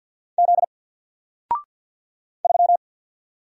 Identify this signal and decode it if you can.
Morse code, heard as a keyed tone, about 50 wpm, 700 Hz. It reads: Z 3
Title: Single Letter-Number 50wpm